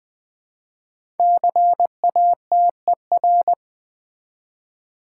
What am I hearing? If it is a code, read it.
Morse code, heard as a keyed tone, about 20 wpm, 700 Hz. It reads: CATER
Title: 5 Letter Words - First 3 Letters Given - 20wpm